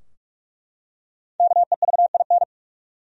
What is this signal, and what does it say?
Morse code, heard as a keyed tone, about 44 wpm, 700 Hz. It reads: KEVIN
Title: USA Common Names - 45wpm